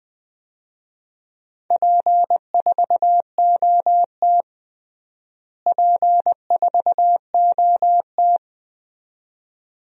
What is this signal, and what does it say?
Morse code, heard as a keyed tone, about 21 wpm, 700 Hz. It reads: P4OT P4OT
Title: DXCC-100 Call Signs - 4 Characters - Repeated 2x - 20wpm